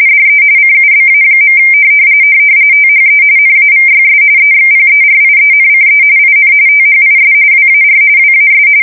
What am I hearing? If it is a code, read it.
RTTY, about 45.45 baud. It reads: DE CUBA. THE NAVYS EXPERIENCE IN THIS WAR WAS
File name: fsk170 / usnavy3